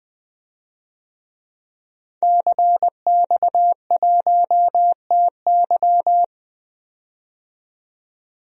Morse code, heard as a keyed tone, about 20 wpm, 700 Hz. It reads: CX1TY
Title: DXCC-100 Call Signs - 5 Characters - 20wpm